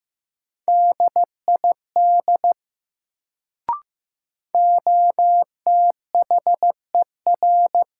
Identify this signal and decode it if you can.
Morse code, heard as a keyed tone, about 15 wpm, 700 Hz. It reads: DID OTHER
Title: Sentences from Top 100 Words - New Word Drill - 15wpm